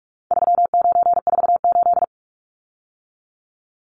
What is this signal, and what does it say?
Morse code, heard as a keyed tone, about 44 wpm, 700 Hz. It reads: 3948
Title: Numbers - 4-Digit 45wpm